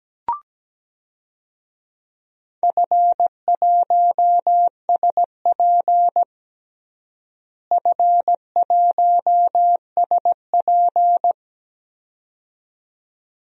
Morse code, heard as a keyed tone, about 17 wpm, 700 Hz. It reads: F1SP F1SP
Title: DXCC-100 Call Signs - 4 Characters - Repeated 2x - 17wpm